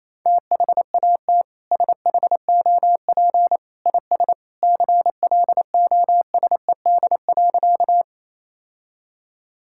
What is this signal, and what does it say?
Morse code, heard as a keyed tone, about 27 wpm, 700 Hz. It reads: THAT SHOP IS CLOSED.
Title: Sentences from Top 1600 Words 28wpm